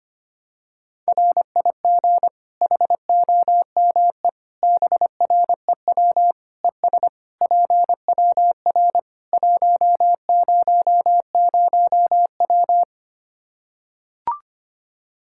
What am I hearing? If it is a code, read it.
Morse code, heard as a keyed tone, about 25 wpm, 700 Hz. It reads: RIG HOME BREW ES PWR 100W
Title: Practice QSO - Exchange 2 - Info - 25wpm